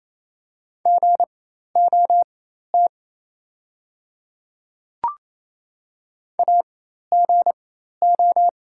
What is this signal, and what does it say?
Morse code, heard as a keyed tone, about 28 wpm, spaced as for 13 wpm, 700 Hz. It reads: GOT AGO
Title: 4x Character Spacing - Top 500 Words - 28wpm